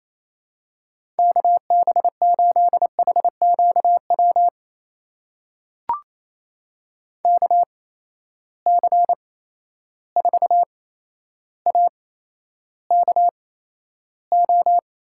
Morse code, heard as a keyed tone, about 28 wpm, spaced as for 12 wpm, 700 Hz. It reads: KB8HQW KC4AKO
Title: USA Call Signs - 6 Characters - 8x Character Spacing - 28wpm